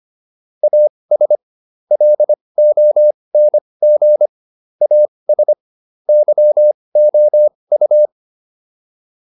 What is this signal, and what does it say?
Morse code, heard as a keyed tone, about 25 wpm, spaced as for 20 wpm, 600 Hz. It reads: AS LONG AS YOU